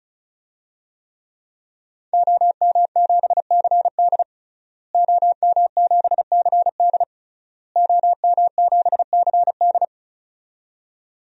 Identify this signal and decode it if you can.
Morse code, heard as a keyed tone, about 36 wpm, 700 Hz. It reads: OM7CD OM7CD OM7CD
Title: DXCC-100 Call Signs - 5 Characters - Repeated 3x - 35wpm